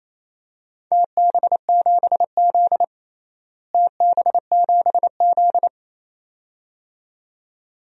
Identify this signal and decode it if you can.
Morse code, heard as a keyed tone, about 28 wpm, 700 Hz. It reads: TB7Z TB7Z
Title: DXCC-100 Call Signs - 4 Characters - Repeated 2x - 28wpm